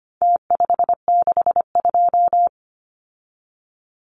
Morse code, heard as a keyed tone, about 25 wpm, 700 Hz. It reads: T562
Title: Numbers - Contest Style - 4-Digit 25wpm